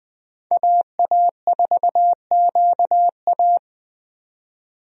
Morse code, heard as a keyed tone, about 20 wpm, 700 Hz. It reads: AA4QA
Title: USA Call Signs - 5 Characters - 20wpm